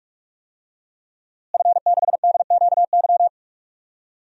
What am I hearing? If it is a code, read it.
Morse code, heard as a keyed tone, about 43 wpm, 700 Hz. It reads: F6DQY
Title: DXCC-100 Call Signs - 5 Characters - Repeated 3x - 45wpm